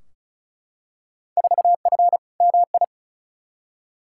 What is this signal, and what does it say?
Morse code, heard as a keyed tone, about 36 wpm, 700 Hz. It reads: 4F MI
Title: Field Day Exchanges - 35wpm